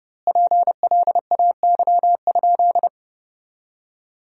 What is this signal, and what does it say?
Morse code, heard as a keyed tone, about 30 wpm, 700 Hz. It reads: PLAY?
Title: 2x Word-Spacing - Sentences from Top 300 Words - 30wpm